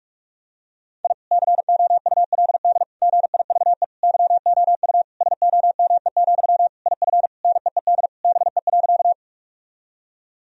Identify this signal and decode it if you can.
Morse code, heard as a keyed tone, about 46 wpm, 700 Hz. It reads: I COULD GIVE YOU SOME, IF NEED BE.
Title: Sentences from Top 1000 Words - Review Part 1 - 45wpm